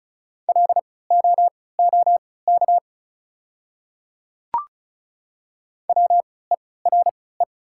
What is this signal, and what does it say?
Morse code, heard as a keyed tone, about 36 wpm, spaced as for 20 wpm, 690 Hz. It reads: LOOK WERE
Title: 3x Character Spacing - Top 200 Words - 35wpm